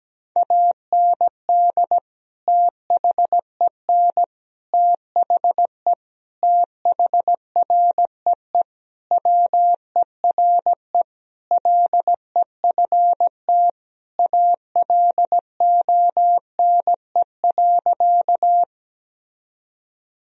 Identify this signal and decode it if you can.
Morse code, heard as a keyed tone, about 17 wpm, 700 Hz. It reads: AND THEN THE THREE WERE LEFT ALONE.
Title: Sentences from Top 1000 Words - Review Part 1 - 17wpm